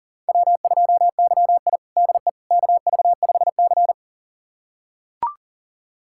Morse code, heard as a keyed tone, about 40 wpm, 700 Hz. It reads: W2YI DE KV5C
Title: USA Callsign DE Callsign - 4 Characters - 40wpm